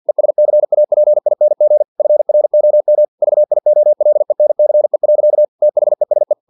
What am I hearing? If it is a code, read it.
Morse code, heard as a keyed tone, about 50 wpm, 600 Hz. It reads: ESCAPING FROM VIOLENCE. THESE
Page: Never 49wpm